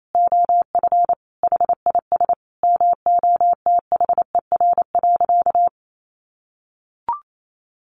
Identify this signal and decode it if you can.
Morse code, heard as a keyed tone, about 28 wpm, 700 Hz. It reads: OF HIS MOTHER.
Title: Sentences from Top 1300 Words 28wpm